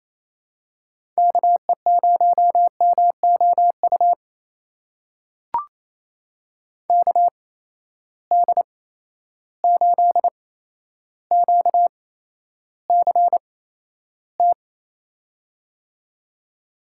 Morse code, heard as a keyed tone, about 28 wpm, spaced as for 12 wpm, 700 Hz. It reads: KE0MOU KD8QCT
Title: USA Call Signs - 6 Characters - 8x Character Spacing - 28wpm